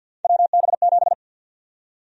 Morse code, heard as a keyed tone, about 52 wpm, 700 Hz. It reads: WB7
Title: USA Call Sign Prefixes - Repeated 2x - 50wpm